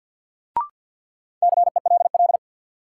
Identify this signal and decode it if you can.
Morse code, heard as a keyed tone, about 53 wpm, 700 Hz. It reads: CELL
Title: Sentences from Top 1100 Words - New Word Drill - 50wpm